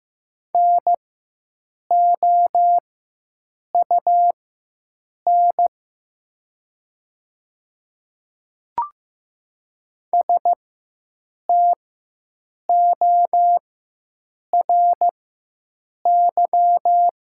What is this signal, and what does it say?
Morse code, heard as a keyed tone, about 15 wpm, spaced as for 7 wpm, 700 Hz. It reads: NOUN STORY
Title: 4x Character Spacing - Top 500 Words - 15wpm